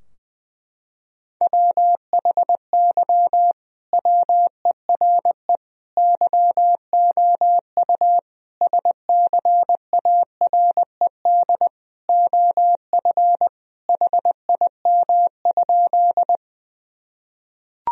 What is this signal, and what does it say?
Morse code, heard as a keyed tone, about 20 wpm, 700 Hz. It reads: WHY WERE YOU SCARED OF HIM?
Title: Sentences from Top 1000 Words - Review Part 4 - 20wpm